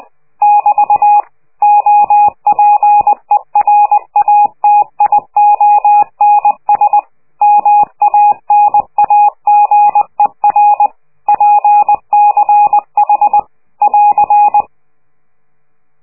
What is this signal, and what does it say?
Morse code, heard as a keyed tone, about 20 wpm, 830 Hz. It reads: <BT> OPERATIONS MANAGER PCH <AR>
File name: worldcoastal / scheveningenradio / morsepch